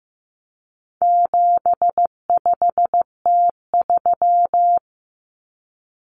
Morse code, heard as a keyed tone, about 15 wpm, 700 Hz. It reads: 75T3
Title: Common Cut Numbers - 4-Digit 15wpm